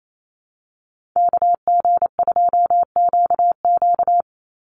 Morse code, heard as a keyed tone, about 28 wpm, 700 Hz. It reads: KG2QQ
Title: USA Call Signs 28wpm